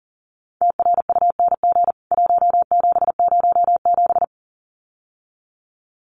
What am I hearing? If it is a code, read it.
Morse code, heard as a keyed tone, about 37 wpm, 700 Hz. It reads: TRUNG 1707
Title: CWT Exchange 40wpm